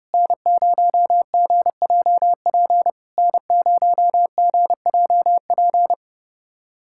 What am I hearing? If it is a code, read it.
Morse code, heard as a keyed tone, about 30 wpm, 700 Hz. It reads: N0GJP N0GJP
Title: SOTA QSO Practice 30wpm